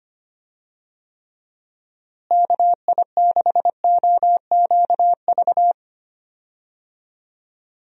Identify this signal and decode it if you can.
Morse code, heard as a keyed tone, about 25 wpm, 700 Hz. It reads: KI6OQV
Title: USA Call Signs - 6 Characters - Repeated 2x - 25wpm